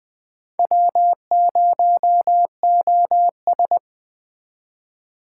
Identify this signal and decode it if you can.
Morse code, heard as a keyed tone, about 24 wpm, 700 Hz. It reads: W0OS
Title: Speed Racing - USA 1x2 Call Signs 20wpm